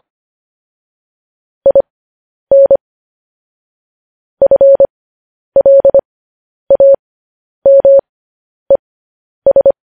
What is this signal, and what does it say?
Morse code, heard as a keyed tone, about 25 wpm, spaced as for 10 wpm, 550 Hz. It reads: IN FLAMES